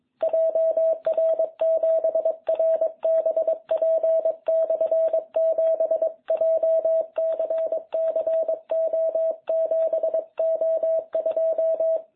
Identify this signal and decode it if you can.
Morse code, heard as a keyed tone, about 22 wpm, 660 Hz. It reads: JR7RBP/7 JCC O7O2